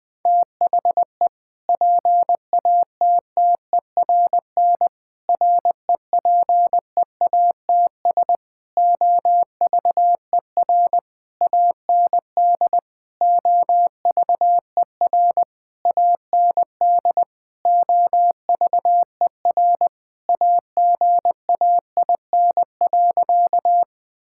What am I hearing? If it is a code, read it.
Morse code, heard as a keyed tone, about 20 wpm, 700 Hz. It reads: THE PATTERN REPEATS OVER AND OVER AND OVER AGAIN.
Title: Sentences with 6-Word Phrases - Spoken Beginning - 20wpm